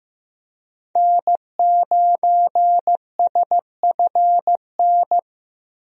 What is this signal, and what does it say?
Morse code, heard as a keyed tone, about 15 wpm, 700 Hz. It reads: N9SFN
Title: USA Call Signs - 5 Characters - Repeated 3x - 15wpm